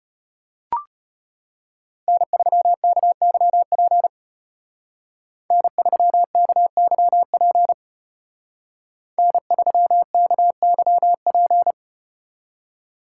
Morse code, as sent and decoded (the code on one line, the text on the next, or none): -. ...-- -.- -.-- .--. / -. ...-- -.- -.-- .--. / -. ...-- -.- -.-- .--.
N3KYP N3KYP N3KYP